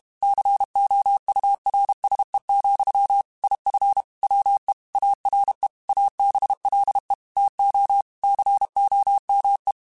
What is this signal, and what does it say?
Morse code, heard as a keyed tone, about 32 wpm, 800 Hz. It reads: COURSE, IF WE ARE ABLE TO COME
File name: QSOs / QSOs-mp3s / WA7QOX 1@30